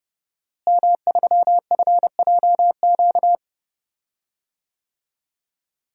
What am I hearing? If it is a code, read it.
Morse code, heard as a keyed tone, about 30 wpm, 700 Hz. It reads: M3FJQ
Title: DXCC-100 Call Signs - 5 Characters - Repeated 2x - 30wpm